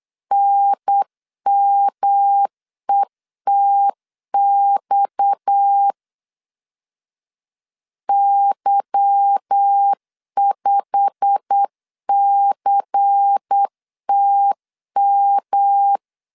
Morse code, heard as a keyed tone, about 8 wpm, 800 Hz. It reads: NMETX Y5CTM
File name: LEZIONI / 10 SIGNS / 10A SLASH 8wpm 3452